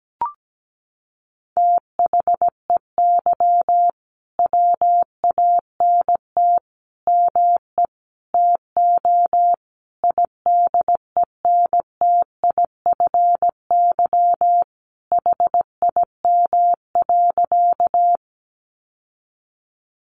Morse code, heard as a keyed tone, about 17 wpm, 700 Hz. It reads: THEY WANT ME TO IDENTIFY HIM.
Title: Sentences from Top 900 Words 17wpm